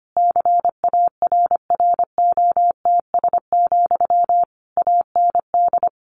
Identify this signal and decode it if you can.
Morse code, heard as a keyed tone, about 25 wpm, 700 Hz. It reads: CARROTS, AND
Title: Easy Phrases 25wpm